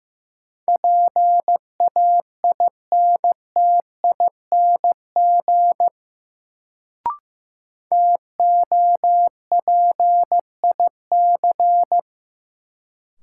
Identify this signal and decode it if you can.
Morse code, heard as a keyed tone, about 15 wpm, 700 Hz. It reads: PAINTING TOPIC
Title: Sentences from Top 1200 Words - New Word Drill Alt - 15wpm